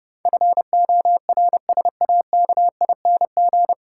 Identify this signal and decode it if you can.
Morse code, heard as a keyed tone, about 30 wpm, 700 Hz. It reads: FORSAKING
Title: Pangram Challenge v2 - 30wpm